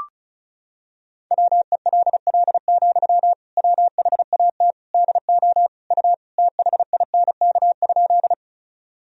Morse code, heard as a keyed tone, about 34 wpm, 690 Hz. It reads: WELL, WHAT DO U THINK?
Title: Sentences from Top 200 Words and Abbreviations 35wpm